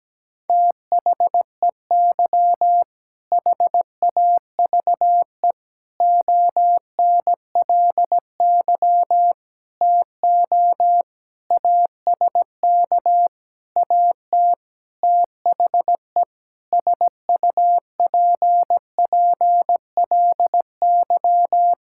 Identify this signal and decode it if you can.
Morse code, heard as a keyed tone, about 17 wpm, 700 Hz. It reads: THEY HAVE ONLY TO ASK AT THE SUPPLY
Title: Sentences from Top 1000 Words - Review Part 5 - 17wpm